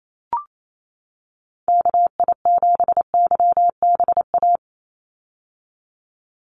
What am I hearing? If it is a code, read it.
Morse code, heard as a keyed tone, about 28 wpm, 700 Hz. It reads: KI7YBA
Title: USA Call Signs 28wpm